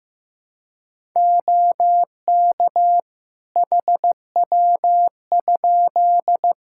Morse code, heard as a keyed tone, about 15 wpm, 700 Hz. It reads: OK HW?
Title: Practice QSO - Exchange 3 - More Info - 15wpm